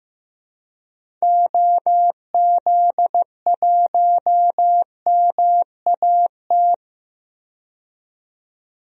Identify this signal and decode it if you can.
Morse code, heard as a keyed tone, about 15 wpm, 700 Hz. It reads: OZ1MAT
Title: DXCC-100 Call Signs - 6 Characters - 15wpm